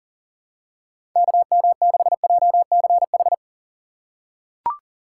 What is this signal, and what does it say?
Morse code, heard as a keyed tone, about 40 wpm, 700 Hz. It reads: KM6JCH
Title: USA Call Signs - 6 Characters - 6x Character Spacing - 40wpm